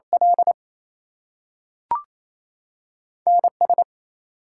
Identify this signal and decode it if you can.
Morse code, heard as a keyed tone, about 28 wpm, 700 Hz. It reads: L NS
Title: US States and Canadian Provinces 28wpm